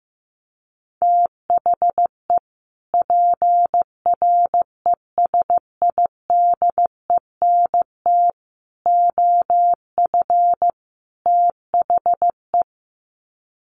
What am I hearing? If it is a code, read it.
Morse code, heard as a keyed tone, about 15 wpm, 700 Hz. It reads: THE PRESIDENT OF THE
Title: Sets of 4 Words 15wpm